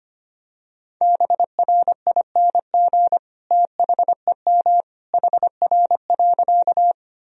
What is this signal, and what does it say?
Morse code, heard as a keyed tone, about 25 wpm, 700 Hz. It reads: BRING THEM HR.
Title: Sentences from Top 500 Words and Abbreviations 25wpm